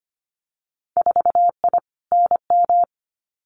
Morse code, heard as a keyed tone, about 25 wpm, 700 Hz. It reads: 4I NM
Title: Winter Field Day Exchanges Encore 25wpm